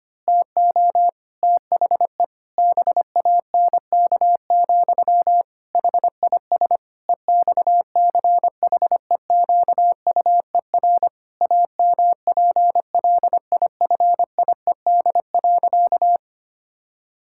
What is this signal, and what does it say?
Morse code, heard as a keyed tone, about 25 wpm, 700 Hz. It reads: TO THE BANK, HIS EXCHEQUER AMPLIFIED.
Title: Pangram Challenge v2 - 25wpm